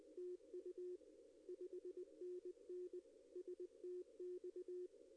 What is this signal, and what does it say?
Morse code, heard as a keyed tone, about 20 wpm, 360 Hz. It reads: TU 5NN STX
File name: contests / arrl-160-k5tr-op-w5tn -1300